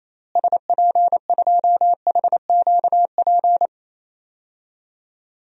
Morse code, heard as a keyed tone, about 28 wpm, 700 Hz. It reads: SP2HQP
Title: DXCC-100 Call Signs - 6 Characters - Repeated 3x - 28wpm